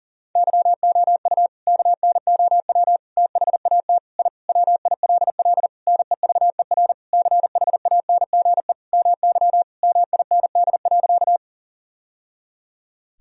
Morse code, heard as a keyed tone, about 40 wpm, 700 Hz. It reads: YOU KNOW THAT I WILL NEVER CHANGE MY MIND.
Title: Sentences from Top 1000 Words - Review Part 1 - 40wpm